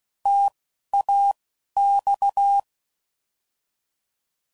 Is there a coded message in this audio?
Yes: Morse code from 0.3 to 2.6 s, heard as a keyed tone, about 16 wpm, spaced as for 11 wpm, 800 Hz. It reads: TAX